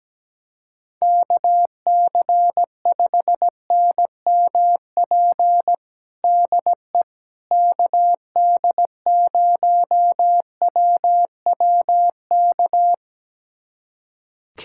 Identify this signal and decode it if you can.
Morse code, heard as a keyed tone, about 17 wpm, 700 Hz. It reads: KC5NMP DE KD0WWK
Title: USA Callsign DE Callsign 17wpm